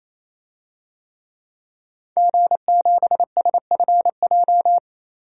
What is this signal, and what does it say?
Morse code, heard as a keyed tone, about 28 wpm, 700 Hz. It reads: G7SFJ
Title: DXCC-100 Call Signs - 5 Characters - 28wpm